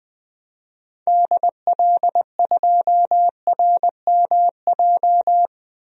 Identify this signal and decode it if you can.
Morse code, heard as a keyed tone, about 20 wpm, 700 Hz. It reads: DL2RMJ
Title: DXCC-100 Call Signs - 6 Characters - 20wpm